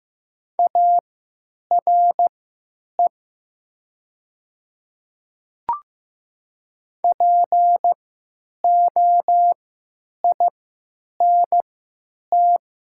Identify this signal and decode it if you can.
Morse code, heard as a keyed tone, about 15 wpm, spaced as for 9 wpm, 700 Hz. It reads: ARE POINT
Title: 3x Character Spacing - Top 200 Words - 15wpm